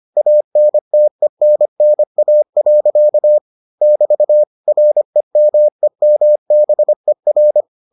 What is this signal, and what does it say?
Morse code, heard as a keyed tone, about 25 wpm, 600 Hz. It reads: ANTENNA. <BT> REMEMBER